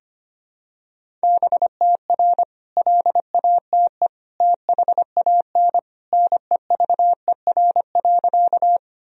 Morse code, heard as a keyed tone, about 25 wpm, 700 Hz. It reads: BTR LATE THAN NEVER.
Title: Sentences from Top 400 Words and Abbreviations 25wpm